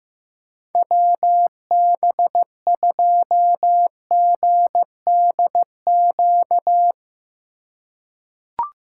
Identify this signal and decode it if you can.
Morse code, heard as a keyed tone, about 15 wpm, 700 Hz. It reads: WB2GDQ
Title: USA Call Signs - 6 Characters - 2x Character Spacing - 15wpm